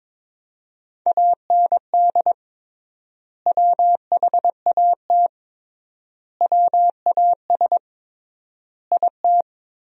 Morse code, heard as a keyed tone, about 22 wpm, 700 Hz. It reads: AND WHAT WAS IT
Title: 3x Word-Spacing - Sentences from Top 100 Words - 22wpm